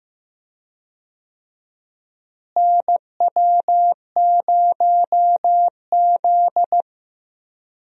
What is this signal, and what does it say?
Morse code, heard as a keyed tone, about 15 wpm, 700 Hz. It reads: NW0Z
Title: USA Call Signs - 4 Characters - Repeated 3x - 15wpm